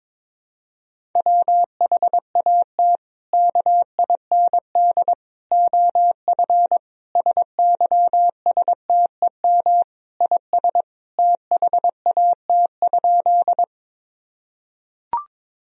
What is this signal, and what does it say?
Morse code, heard as a keyed tone, about 22 wpm, 700 Hz. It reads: WHAT KIND OF SYSTEM IS THAT?
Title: Sentences from Top 1000 Words - Review Part 3 - 22wpm